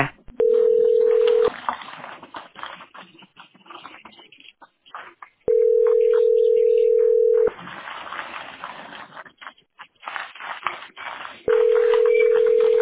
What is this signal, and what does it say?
Ringback tone (North America), 440 + 480 Hz.